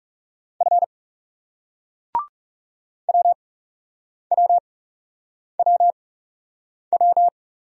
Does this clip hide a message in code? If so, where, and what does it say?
Yes: Morse code from 0.6 to 7.3 s, heard as a keyed tone, about 41 wpm, spaced as for 8 wpm, 700 Hz. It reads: F WWWW